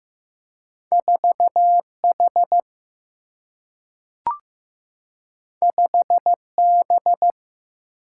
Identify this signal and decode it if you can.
Morse code, heard as a keyed tone, about 15 wpm, 700 Hz. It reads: 4H 5B
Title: Single-Letter Single-Number 15wpm